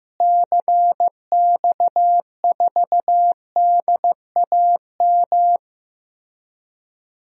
Morse code, heard as a keyed tone, about 15 wpm, 700 Hz. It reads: CX4DAM
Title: DXCC-100 Call Signs - 6 Characters - Repeated 2x - 15wpm